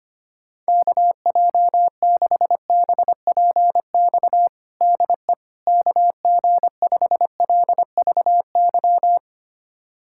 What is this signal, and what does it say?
Morse code, heard as a keyed tone, about 25 wpm, 700 Hz. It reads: KJ6BPX DE KG5LVY
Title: USA Callsign DE Callsign - 6 Characters - 25wpm